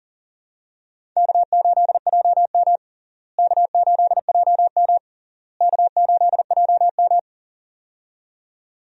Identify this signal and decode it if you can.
Morse code, heard as a keyed tone, about 40 wpm, 700 Hz. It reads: K8JM K8JM K8JM
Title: USA Call Signs - 4 Characters - Repeated 3x - 40wpm